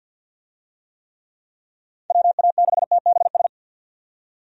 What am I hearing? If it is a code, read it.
Morse code, heard as a keyed tone, about 50 wpm, 700 Hz. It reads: WA6TBS
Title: USA Call Signs - 6 Characters - 50wpm